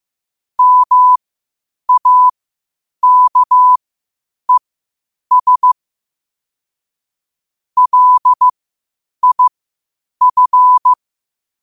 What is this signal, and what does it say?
Morse code, heard as a keyed tone, about 15 wpm, spaced as for 8 wpm, 1000 Hz. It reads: MAKES LIF
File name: qotdfiles / 8wpm